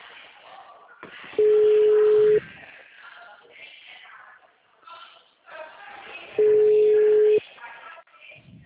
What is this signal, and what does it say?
Ringback tone (Europe, 425 Hz), with speech.
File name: out- -2038- - - .48825